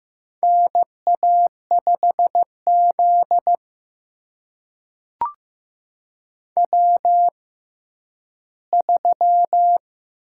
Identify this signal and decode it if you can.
Morse code, heard as a keyed tone, about 15 wpm, spaced as for 10 wpm, 700 Hz. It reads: NA5Z W3
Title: USA Call Signs - 4 Characters - 6x Character Spacing - 15wpm